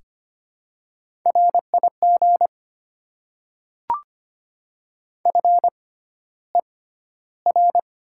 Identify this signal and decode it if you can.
Morse code, heard as a keyed tone, about 25 wpm, spaced as for 13 wpm, 690 Hz. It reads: RIG FER
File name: Top 20 QSO Elements - 6x Character Spacing - 25wpm